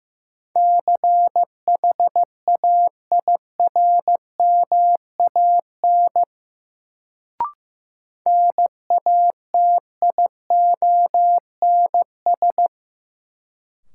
Morse code, heard as a keyed tone, about 15 wpm, 700 Hz. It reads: CHAIRMAN NATIONS
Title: Sets of 3 Words - New Word Drill - 15wpm